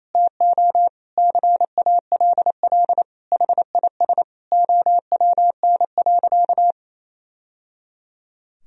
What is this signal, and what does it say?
Morse code, heard as a keyed tone, about 28 wpm, 700 Hz. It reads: TO CALL HIS OWN.
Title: Sentences from Top 1000 Words - Review Part 2 - 28wpm